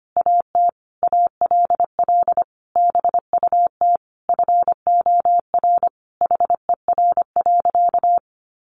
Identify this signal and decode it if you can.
Morse code, heard as a keyed tone, about 25 wpm, 700 Hz. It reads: AT ALL BUT FOR HER.
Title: Sentences from Top 1400 Words 25wpm